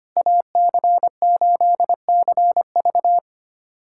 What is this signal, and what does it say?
Morse code, heard as a keyed tone, about 25 wpm, 700 Hz. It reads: AC8CV
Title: USA Callsign DE Callsign - 5 Characters - 25wpm